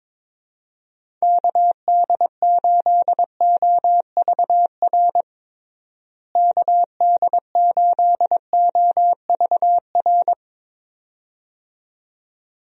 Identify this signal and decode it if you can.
Morse code, heard as a keyed tone, about 22 wpm, 700 Hz. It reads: KD8OVR KD8OVR
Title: USA Call Signs - 6 Characters - Repeated 2x - 22wpm